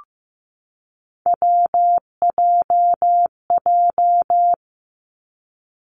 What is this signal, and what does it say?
Morse code, heard as a keyed tone, about 15 wpm, 700 Hz. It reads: WJJ
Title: Three Challenging Letter-Number Combinations 15wpm